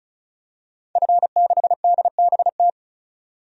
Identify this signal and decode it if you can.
Morse code, heard as a keyed tone, about 34 wpm, 700 Hz. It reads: F6DBT
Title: DXCC-100 Call Signs - 5 Characters - 35wpm